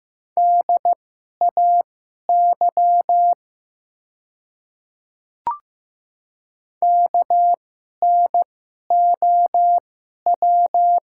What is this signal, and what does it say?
Morse code, heard as a keyed tone, about 15 wpm, spaced as for 11 wpm, 690 Hz. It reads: DAY KNOW
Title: 2x Character Spacing - Top 100 Words - 15wpm